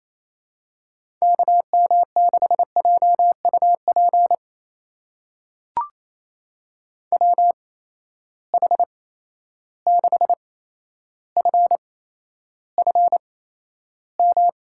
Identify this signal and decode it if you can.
Morse code, heard as a keyed tone, about 28 wpm, spaced as for 12 wpm, 700 Hz. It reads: KM6JUP WH6FFM
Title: USA Call Signs - 6 Characters - 8x Character Spacing - 28wpm